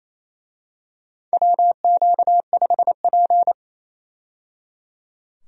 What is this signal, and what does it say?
Morse code, heard as a keyed tone, about 28 wpm, 700 Hz. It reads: WQ5P
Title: USA Call Signs - 4 Characters - 28wpm